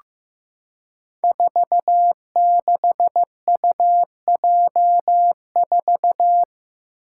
Morse code, heard as a keyed tone, about 15 wpm, 700 Hz. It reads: 46UJ4
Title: Five Challenging Letter-Number Combinations 15wpm